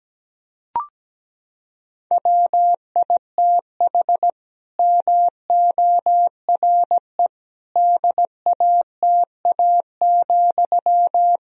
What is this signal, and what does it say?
Morse code, heard as a keyed tone, about 17 wpm, 700 Hz. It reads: WITH MORE DATA,
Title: Sentences from Top 1000 Words - Review Part 3 - 17wpm